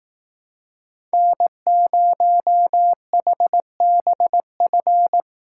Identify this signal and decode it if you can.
Morse code, heard as a keyed tone, about 18 wpm, 700 Hz. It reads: N0HBF
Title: Speed Racing - USA 1x3 Call Signs 15wpm